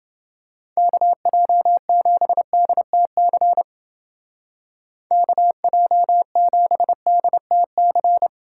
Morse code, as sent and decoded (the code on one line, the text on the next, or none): -.- .--- --... -.. - -.-. / -.- .--- --... -.. - -.-.
KJ7DTC KJ7DTC